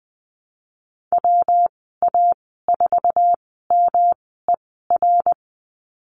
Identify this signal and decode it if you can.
Morse code, heard as a keyed tone, about 20 wpm, spaced as for 15 wpm, 700 Hz. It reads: WA4MER
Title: 173 Lesson 28g - Number 9 - Call Signs 20wpm